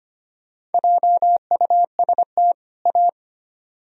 Morse code, heard as a keyed tone, about 34 wpm, spaced as for 30 wpm, 700 Hz. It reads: JUST A
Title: Speed-Racing - Sets of 2 Words 25wpm